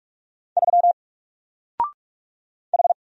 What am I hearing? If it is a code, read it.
Morse code, heard as a keyed tone, about 46 wpm, 700 Hz. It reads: 3 H
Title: Single Character (Letter:Number:Punctuation:Prosign) 45wpm